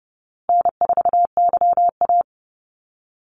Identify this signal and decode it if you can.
Morse code, heard as a keyed tone, about 30 wpm, 700 Hz. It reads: N4YA
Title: Easy Call Signs Repeated 3x - 30wpm